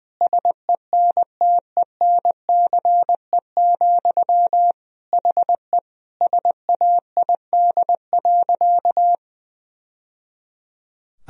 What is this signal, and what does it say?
Morse code, heard as a keyed tone, about 20 wpm, 700 Hz. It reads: SENTENCE, HE SAID.
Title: Sentences from Top 1000 Words - Review Part 1 - 20wpm